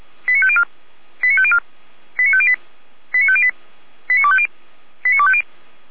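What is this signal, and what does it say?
Five-tone selective call (ZVEI), heard as quick runs of five tones, tones of about 70 ms. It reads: 89504 89504 89508 89508 89250 89250
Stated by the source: ZVEI five-tone selective call received by SDR, Feuerwehr Orlamünde (f85360).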